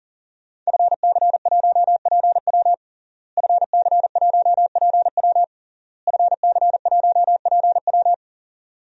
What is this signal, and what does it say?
Morse code, heard as a keyed tone, about 40 wpm, 700 Hz. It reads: FC1PW FC1PW FC1PW
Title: DXCC-100 Call Signs - 5 Characters - Repeated 3x - 40wpm